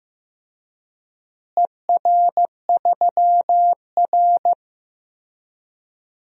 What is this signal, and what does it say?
Morse code, heard as a keyed tone, about 15 wpm, 700 Hz. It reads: ER3R
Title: DXCC-100 Call Signs - 4 Characters - Repeated 3x - 15wpm